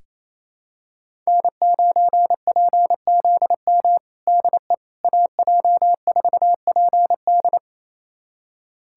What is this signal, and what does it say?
Morse code, heard as a keyed tone, about 28 wpm, 700 Hz. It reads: N9PZM DE AJ4PD
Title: USA Callsign DE Callsign - 5 Characters - 28wpm